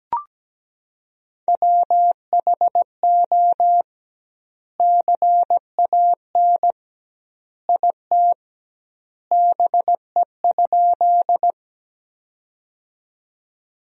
Morse code, heard as a keyed tone, about 17 wpm, 700 Hz. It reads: WHO CAN IT BE?
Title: 2x Word-Spacing - Sentences from Top 100 Words - 17wpm